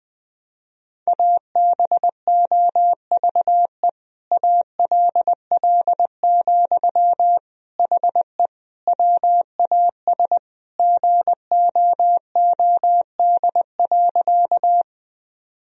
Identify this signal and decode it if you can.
Morse code, heard as a keyed tone, about 20 wpm, 700 Hz. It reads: ABOVE ALL, HE WAS GOOD.
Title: Sentences from Top 1000 Words - Review Part 2 - 20wpm